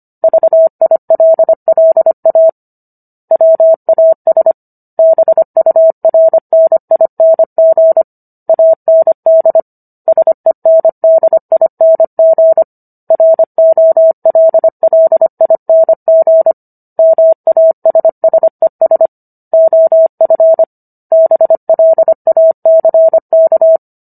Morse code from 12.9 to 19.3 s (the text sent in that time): ROLLING MASSES